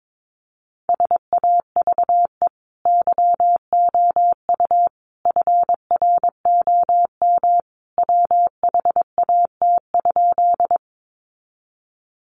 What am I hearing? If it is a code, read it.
Morse code, heard as a keyed tone, about 22 wpm, 700 Hz. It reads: SAVE YOU FROM WHAT?
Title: Sentences from Top 1100 Words 22wpm